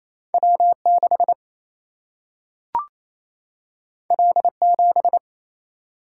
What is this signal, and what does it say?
Morse code, heard as a keyed tone, about 28 wpm, 700 Hz. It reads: W 6 L 7
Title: Single-Letter Single-Number 28wpm